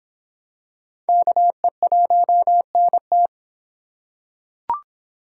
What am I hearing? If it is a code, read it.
Morse code, heard as a keyed tone, about 26 wpm, 700 Hz. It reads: KE1NT